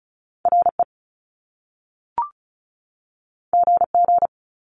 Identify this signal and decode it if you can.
Morse code, heard as a keyed tone, about 36 wpm, 700 Hz. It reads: RE GG
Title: Top Two Letter Combinations in Words 35wpm